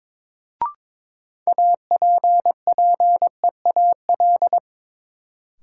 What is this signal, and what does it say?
Morse code, heard as a keyed tone, about 22 wpm, 700 Hz. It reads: APPEAL
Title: Sentences from Top 1300 Words - New Word Drill Alt - 22wpm